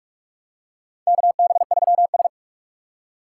Morse code, heard as a keyed tone, about 45 wpm, 700 Hz. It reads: KB3S
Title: USA Call Signs - 4 Characters - 6x Character Spacing - 45wpm